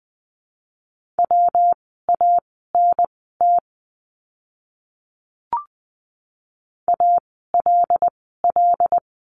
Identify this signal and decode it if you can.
Morse code, heard as a keyed tone, about 20 wpm, spaced as for 14 wpm, 700 Hz. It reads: WANT ALL
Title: 115 Lesson 19f - Letter w - Words 20wpm